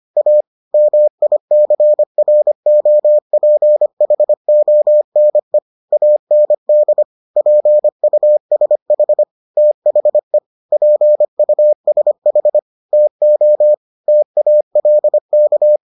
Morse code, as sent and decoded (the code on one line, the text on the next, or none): .- / -- .. -.-. .-. --- .--. .... --- -. / .- -. -.. / .--. ..- ... .... / - .... / .--. ..- ... .... / - --- / - .- .-.. -.-
A MICROPHONE AND PUSH THE PUSH TO TALK